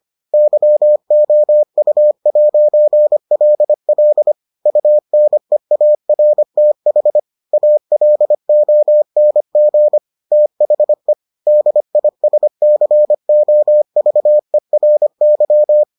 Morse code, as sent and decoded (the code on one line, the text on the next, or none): -.-- --- ..- .----. .-.. .-.. / ..- -. .- .-. - .... / .- .-.. --- -. --. / - .... / -.. .. ... -.-. --- ...- . .-. -.--
YOU'LL UNEARTH ALONG THE DISCOVERY